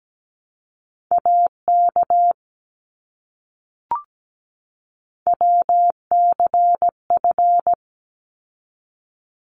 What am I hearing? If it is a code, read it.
Morse code, heard as a keyed tone, about 17 wpm, 700 Hz. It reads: AK WCF
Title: ARRL Sections 17wpm